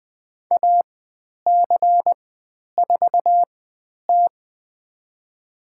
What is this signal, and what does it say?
Morse code, heard as a keyed tone, about 20 wpm, spaced as for 10 wpm, 700 Hz. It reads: AC4T
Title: 109 Lesson 18g - Letter m - Call Signs 20wpm